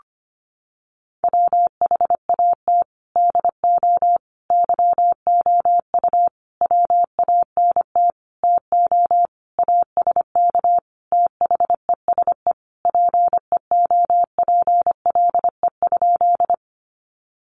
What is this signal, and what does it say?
Morse code, heard as a keyed tone, about 25 wpm, 700 Hz. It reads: WHAT DO YOU WANT TO ASK THESE PEOPLE?
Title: Sentences from Top 200 Words 25wpm